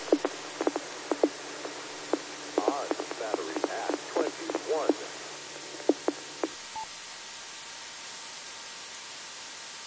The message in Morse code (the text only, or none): none